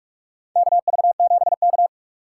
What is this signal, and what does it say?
Morse code, heard as a keyed tone, about 43 wpm, 700 Hz. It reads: KV7K
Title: USA Call Signs - 4 Characters - 6x Character Spacing - 45wpm